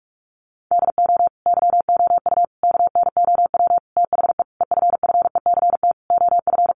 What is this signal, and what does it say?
Morse code, heard as a keyed tone, about 44 wpm, 700 Hz. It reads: DO YOU KNOW THE EFFECT OF
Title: Sentences from Top 1400 Words 45wpm